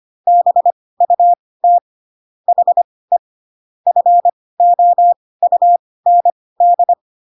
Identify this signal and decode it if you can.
Morse code, heard as a keyed tone, about 25 wpm, spaced as for 18 wpm, 700 Hz. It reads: BUT HE FOUND